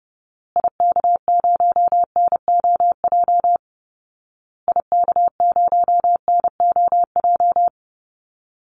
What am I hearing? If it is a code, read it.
Morse code, heard as a keyed tone, about 30 wpm, 700 Hz. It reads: IK0NOJ IK0NOJ
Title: Call Signs Repeated 2x - 30wpm